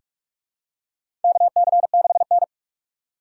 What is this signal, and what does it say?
Morse code, heard as a keyed tone, about 46 wpm, 700 Hz. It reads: KC6N
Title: USA Call Signs - 6x Character Spacing - 45wpm